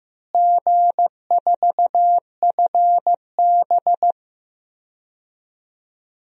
Morse code, heard as a keyed tone, about 15 wpm, 700 Hz. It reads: G4FB
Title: DXCC-100 Call Signs - 4 Characters - Repeated 2x - 15wpm